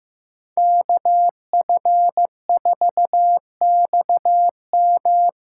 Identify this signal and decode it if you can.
Morse code, heard as a keyed tone, about 16 wpm, 700 Hz. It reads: KF4XM
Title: USA Callsign DE Callsign - 5 Characters - 15wpm